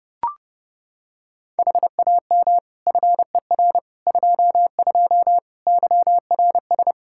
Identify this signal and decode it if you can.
Morse code, heard as a keyed tone, about 32 wpm, 700 Hz. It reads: HAM FER 22 YRS
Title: Practice QSO - Exchange 3 - More Info - 30wpm